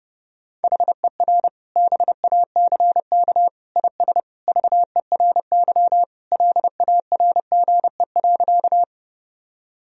Morse code, heard as a keyed tone, about 30 wpm, 700 Hz. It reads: HER BACK IS VERY LARGE.
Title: Sentences from Top 1000 Words - Review Part 1 - 30wpm